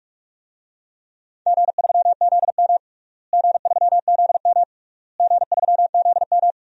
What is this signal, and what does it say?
Morse code, heard as a keyed tone, about 46 wpm, 700 Hz. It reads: G3ZM G3ZM G3ZM
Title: DXCC-100 Call Signs - 4 Characters - Repeated 3x - 45wpm